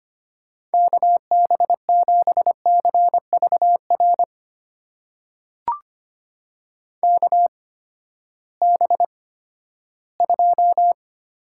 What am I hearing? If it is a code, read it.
Morse code, heard as a keyed tone, about 25 wpm, spaced as for 14 wpm, 700 Hz. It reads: KB7CVR KB2
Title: USA Call Signs - 6 Characters - 8x Character Spacing - 25wpm